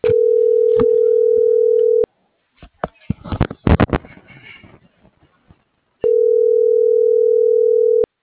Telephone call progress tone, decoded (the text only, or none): ringback tone